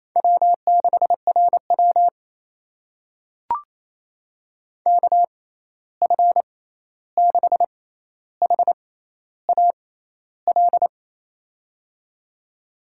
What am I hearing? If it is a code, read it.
Morse code, heard as a keyed tone, about 28 wpm, spaced as for 13 wpm, 700 Hz. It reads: W6RW KF6HAL
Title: USA Call Signs - 6x Character Spacing - 28wpm